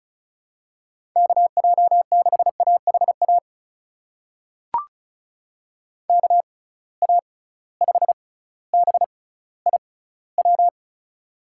Morse code, heard as a keyed tone, about 35 wpm, spaced as for 18 wpm, 700 Hz. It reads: KJ6AHA KA5BIW
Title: USA Call Signs - 6 Characters - 6x Character Spacing - 35wpm